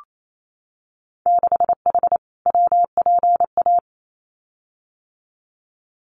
Morse code, heard as a keyed tone, about 28 wpm, 700 Hz. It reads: 6H WPA
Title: Winter Field Day Exchanges Encore 28wpm